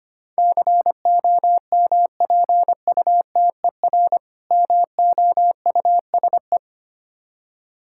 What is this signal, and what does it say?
Morse code, heard as a keyed tone, about 26 wpm, 700 Hz. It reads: COMPUTER MOUSE
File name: practice / 25wpm / Difficult Words - Speed-Racing with Hints - 25wpm